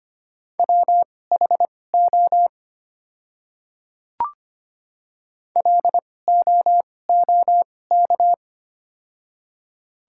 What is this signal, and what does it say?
Morse code, heard as a keyed tone, about 25 wpm, spaced as for 18 wpm, 690 Hz. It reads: WHO LOOK
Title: 2x Character Spacing - Top 300 Words - 25wpm